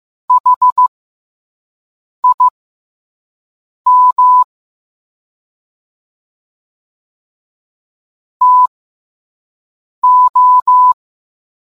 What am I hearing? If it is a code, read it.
Morse code, heard as a keyed tone, about 15 wpm, spaced as for 5 wpm, 1000 Hz. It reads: HIM TO